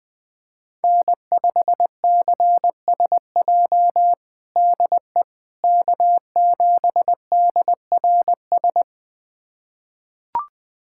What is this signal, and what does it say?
Morse code, heard as a keyed tone, about 20 wpm, 700 Hz. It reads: N5CSJ DE K7DRS
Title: USA Callsign DE Callsign - 5 Characters - 20wpm